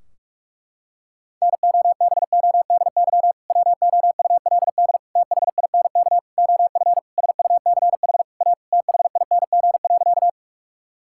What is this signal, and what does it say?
Morse code, heard as a keyed tone, about 41 wpm, 700 Hz. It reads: NOBODY WOULD THINK OF SUCH A THING.
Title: Sentences from Top 1600 Words 45wpm